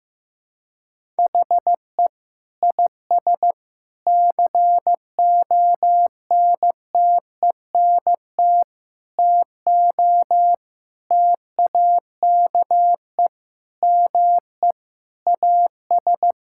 Morse code, heard as a keyed tone, about 15 wpm, 700 Hz. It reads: HE IS CONTENT TO TAKE ME AS
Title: Sentences from Top 1000 Words - Review Part 6 - 15wpm